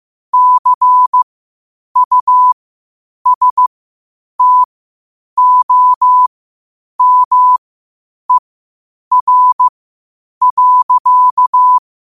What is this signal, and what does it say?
Morse code, heard as a keyed tone, about 15 wpm, spaced as for 8 wpm, 1000 Hz. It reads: CUSTOMER.
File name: qotdfiles / 8wpm